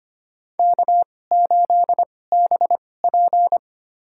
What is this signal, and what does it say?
Morse code, heard as a keyed tone, about 25 wpm, spaced as for 18 wpm, 700 Hz. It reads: K8BP
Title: USA Call Signs - 4 Characters - 2x Character Spacing - 25wpm